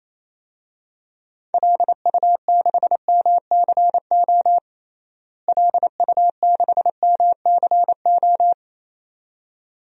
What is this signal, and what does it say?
Morse code, heard as a keyed tone, about 28 wpm, 700 Hz. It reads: LU6MCO LU6MCO
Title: DXCC-100 Call Signs - 6 Characters - Repeated 2x - 28wpm